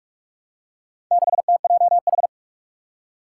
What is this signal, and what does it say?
Morse code, heard as a keyed tone, about 46 wpm, 700 Hz. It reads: 6TJH
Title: Four Challenging Letter-Number Combinations 45wpm